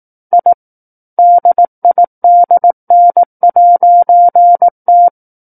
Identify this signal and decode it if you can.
Morse code, heard as a keyed tone, about 18 wpm, 700 Hz. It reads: I DIDN'T
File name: Book One / 09-Chapter 9 - 18 WPM